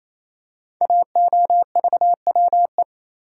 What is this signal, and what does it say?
Morse code, heard as a keyed tone, about 27 wpm, 700 Hz. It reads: AOVWE
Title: Code Groups - 5 Letters - 28wpm